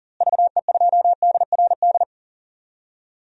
Morse code, heard as a keyed tone, about 37 wpm, 700 Hz. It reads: VE2DRD
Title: DXCC-100 Call Signs - 6 Characters - Repeated 3x - 40wpm